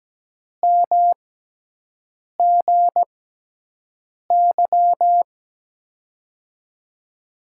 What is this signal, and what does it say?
Morse code, heard as a keyed tone, about 17 wpm, spaced as for 6 wpm, 700 Hz. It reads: MGY
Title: USA Call Signs - 5 Characters - 6x Character Spacing - 17wpm